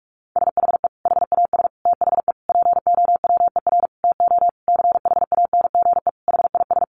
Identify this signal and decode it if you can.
Morse code, heard as a keyed tone, about 44 wpm, 700 Hz. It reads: SHE HAS THE POWER TO CHANGE HIS
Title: Sentences from Top 1200 Words 45wpm